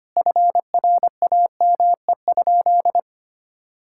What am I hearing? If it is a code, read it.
Morse code, heard as a keyed tone, about 25 wpm, 700 Hz. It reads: FRAME?
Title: Sentences from Top 1000 Words - Review Part 6 - 25wpm